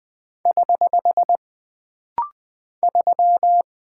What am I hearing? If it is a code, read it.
Morse code, heard as a keyed tone, about 20 wpm, 700 Hz. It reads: <HH> 3
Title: Single Character (Letter:Number:Punctuation:Prosign) - Rapid-Fire - 20wpm